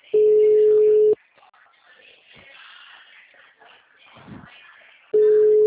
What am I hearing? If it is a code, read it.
Ringback tone (Europe, 425 Hz).
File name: out- -2011- - - .75404